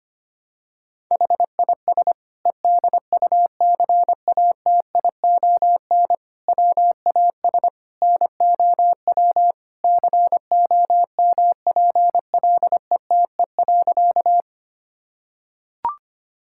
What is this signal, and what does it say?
Morse code, heard as a keyed tone, about 25 wpm, 700 Hz. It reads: HIS EDUCATION WAS NOW COMPLETE.
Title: Sentences from Top 1000 Words - Review Part 6 - 25wpm